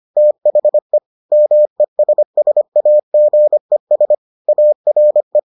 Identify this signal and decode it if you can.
Morse code, heard as a keyed tone, about 25 wpm, 600 Hz. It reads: THE MESSAGES ARE